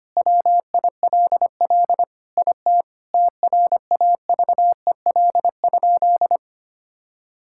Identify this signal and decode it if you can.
Morse code, heard as a keyed tone, about 25 wpm, 700 Hz. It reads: WILL IT TRAVEL?
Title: Sentences from Top 1000 Words - Review Part 6 - 25wpm